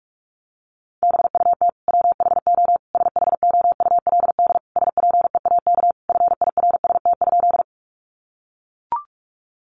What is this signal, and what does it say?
Morse code, heard as a keyed tone, about 44 wpm, 700 Hz. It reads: BUT WHO SHOULD SPEAK FIRST?
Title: Sentences from Top 1100 Words 45wpm